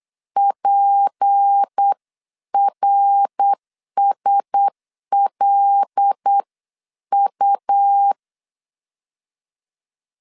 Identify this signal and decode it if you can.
Morse code, heard as a keyed tone, about 8 wpm, 800 Hz. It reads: PRSLU
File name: 8wpm